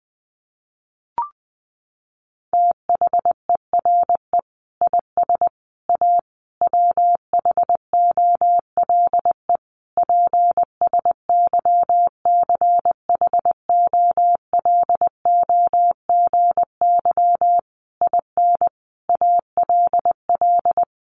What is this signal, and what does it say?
Morse code, heard as a keyed tone, about 20 wpm, 700 Hz. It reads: THERE IS A WHOLE PSYCHOLOGY IN ALL
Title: Sentences from Top 1400 Words 20wpm